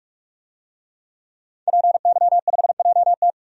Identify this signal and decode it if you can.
Morse code, heard as a keyed tone, about 46 wpm, 700 Hz. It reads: PY5JT
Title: DXCC-100 Call Signs - 5 Characters - 45wpm